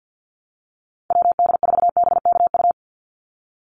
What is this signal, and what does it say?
Morse code, heard as a keyed tone, about 50 wpm, 700 Hz. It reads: WD4BKU
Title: USA Call Signs 50wpm